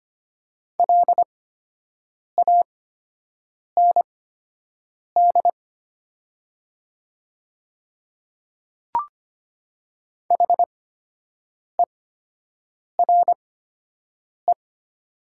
Morse code, heard as a keyed tone, about 25 wpm, spaced as for 7 wpm, 700 Hz. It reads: LAND HERE
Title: 8x Character Spacing - Top 200 Words - 25wpm